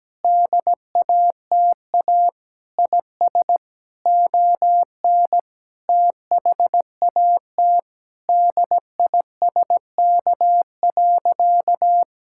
Morse code, heard as a keyed tone, about 17 wpm, 700 Hz. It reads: DATA IS ON THAT DISK.
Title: Sentences from Top 1000 Words - Review Part 6 - 17wpm